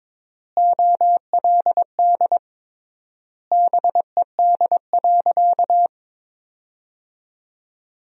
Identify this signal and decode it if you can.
Morse code, heard as a keyed tone, about 22 wpm, 700 Hz. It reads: OLD BED.
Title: 3x Word-Spacing - Sentences from Top 500 Words - 22wpm